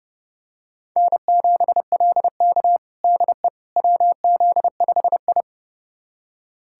Morse code, heard as a keyed tone, about 32 wpm, 700 Hz. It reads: N7LK DE WZ5I
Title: USA Callsign DE Callsign - 4 Characters - 30wpm